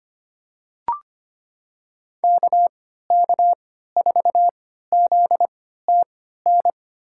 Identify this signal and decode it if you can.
Morse code, heard as a keyed tone, about 25 wpm, spaced as for 14 wpm, 700 Hz. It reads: KK4ZTN
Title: USA Call Signs - 6 Characters - 3x Character Spacing - 25wpm